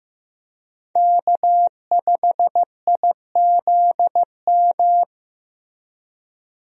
Morse code, heard as a keyed tone, about 15 wpm, 700 Hz. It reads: K5IZM
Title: USA Call Signs - 5 Characters - Repeated 3x - 15wpm